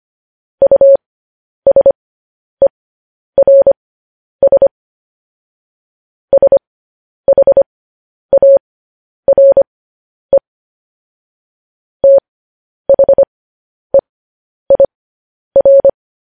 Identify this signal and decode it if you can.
Morse code, heard as a keyed tone, about 25 wpm, spaced as for 10 wpm, 550 Hz. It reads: USERS SHARE THEIR
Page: news headlines in morse code at 10 wpm